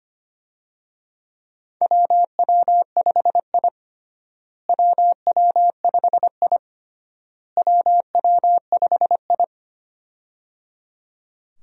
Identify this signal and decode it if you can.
Morse code, heard as a keyed tone, about 25 wpm, 700 Hz. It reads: WW5I WW5I WW5I
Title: USA Call Signs - 4 Characters - Repeated 3x - 25wpm